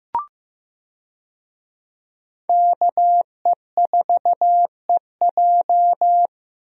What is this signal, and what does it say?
Morse code, heard as a keyed tone, about 15 wpm, 700 Hz. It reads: KE4EJ
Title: USA Call Signs - 5 Characters - Repeated 3x - 15wpm